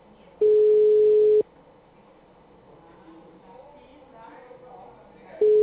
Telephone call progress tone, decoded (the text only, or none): ringback tone